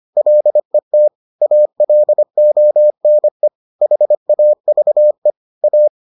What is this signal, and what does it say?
Morse code, heard as a keyed tone, about 25 wpm, 600 Hz. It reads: LET ALONE HAVE A